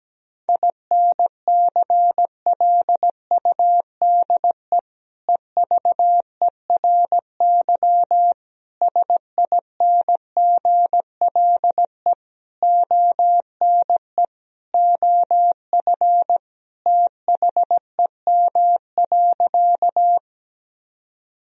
Morse code, heard as a keyed tone, about 17 wpm, 700 Hz. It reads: INCLUDE EVERY SINGLE ONE OF THEM.
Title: Sentences from Top 1000 Words - Review Part 4 - 17wpm